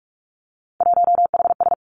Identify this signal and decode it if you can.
Morse code, heard as a keyed tone, about 46 wpm, 700 Hz. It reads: 1HS
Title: Three Challenging Letter-Number Combinations 45wpm